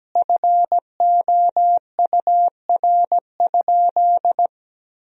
Morse code, heard as a keyed tone, about 17 wpm, 700 Hz. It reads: FOUR?
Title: Sentences from Top 1000 Words - Review Part 1 - 17wpm